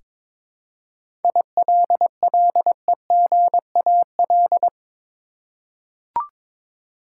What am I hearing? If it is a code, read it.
Morse code, heard as a keyed tone, about 22 wpm, 700 Hz. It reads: ILLEGAL
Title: Seven Letter Words - 22wpm